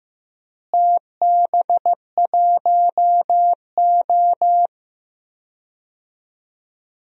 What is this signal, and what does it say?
Morse code, heard as a keyed tone, about 15 wpm, 700 Hz. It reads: TB1O
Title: DXCC-100 Call Signs - 4 Characters - Repeated 3x - 15wpm